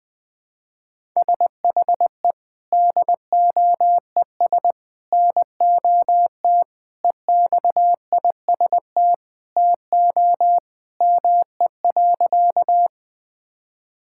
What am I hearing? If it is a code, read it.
Morse code, heard as a keyed tone, about 20 wpm, 700 Hz. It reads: SHE DOES NOT EXIST TO ME.
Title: Sentences from Top 1000 Words - Review Part 5 - 20wpm